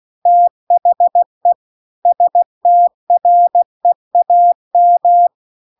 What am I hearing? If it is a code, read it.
Morse code, heard as a keyed tone, about 16 wpm, 700 Hz. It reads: THE STREAM
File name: CW-The-Big-Twoheaded-River-1-16WPM